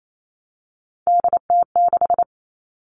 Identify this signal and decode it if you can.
Morse code, heard as a keyed tone, about 28 wpm, 700 Hz. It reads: DT6
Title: All Cut Numbers - 3-Digit 28wpm